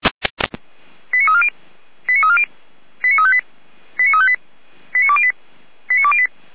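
Five-tone selective call (ZVEI), heard as quick runs of five tones, tones of about 70 ms. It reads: 89340 89340 89358 89358 89208 89208
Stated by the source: ZVEI five-tone selective call 89303 received by SDR.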